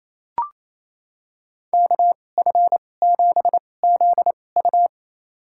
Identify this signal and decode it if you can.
Morse code, heard as a keyed tone, about 28 wpm, spaced as for 20 wpm, 700 Hz. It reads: KF7ZU
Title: USA Call Signs - 5 Characters - 2x Character Spacing - 28wpm